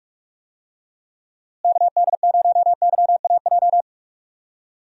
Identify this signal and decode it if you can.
Morse code, heard as a keyed tone, about 46 wpm, 700 Hz. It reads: KD0YAJ
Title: USA Call Signs - 6 Characters - Repeated 3x - 45wpm